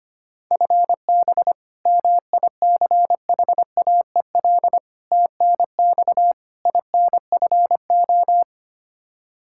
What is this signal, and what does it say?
Morse code, heard as a keyed tone, about 25 wpm, 700 Hz. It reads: FB MICHAEL TNX INFO
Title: Practice QSO - Exchange 3 - More Info - 25wpm